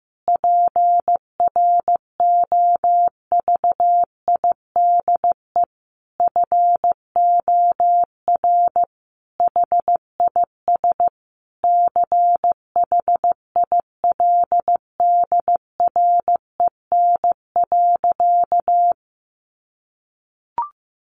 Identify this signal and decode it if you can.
Morse code, heard as a keyed tone, about 15 wpm, 700 Hz. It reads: PROVIDE FOR HIS CHILDREN.
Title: Sentences from Top 1100 Words 15wpm